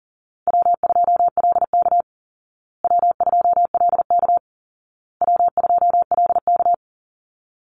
Morse code, heard as a keyed tone, about 40 wpm, 700 Hz. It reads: W2LK W2LK W2LK
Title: Call Signs Repeated 2x - 40wpm